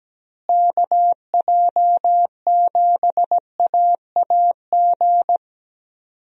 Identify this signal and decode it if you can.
Morse code, heard as a keyed tone, about 17 wpm, 700 Hz. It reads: KJ7AAG
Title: USA Call Signs - 6 Characters - Repeated 2x - 17wpm